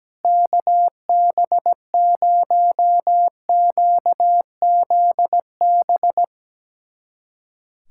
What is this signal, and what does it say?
Morse code, heard as a keyed tone, about 17 wpm, 700 Hz. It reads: KB0QZB
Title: USA Callsign DE Callsign 17wpm